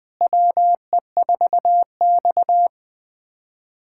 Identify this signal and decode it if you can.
Morse code, heard as a keyed tone, about 21 wpm, 700 Hz. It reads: WE4X
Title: USA Callsign DE Callsign - 4 Characters - 20wpm